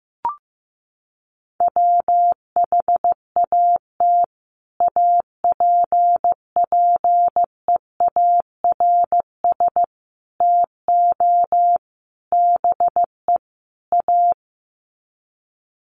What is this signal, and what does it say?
Morse code, heard as a keyed tone, about 15 wpm, 700 Hz. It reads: WHAT APPEARS TO BE A
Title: Sets of 5 Words Encore2 15wpm